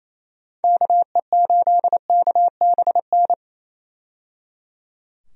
Morse code, heard as a keyed tone, about 28 wpm, 700 Hz. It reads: KE8KBN
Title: USA Call Signs - 6 Characters - 28wpm